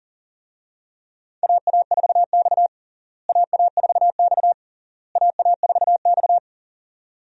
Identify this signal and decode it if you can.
Morse code, heard as a keyed tone, about 37 wpm, 700 Hz. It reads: AA4X AA4X AA4X
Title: USA Call Signs - 4 Characters - Repeated 3x - 40wpm